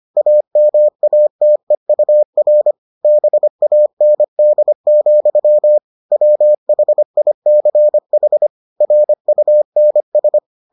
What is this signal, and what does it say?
Morse code, heard as a keyed tone, about 25 wpm, 600 Hz. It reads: AMATEUR BAND, WHICH RUNS